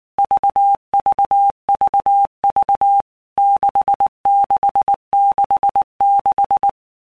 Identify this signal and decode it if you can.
Morse code, heard as a keyed tone, about 19 wpm, 800 Hz. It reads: VVVV 6666
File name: Basic Letter Groups By Blocks / Groups@18 / VB46@18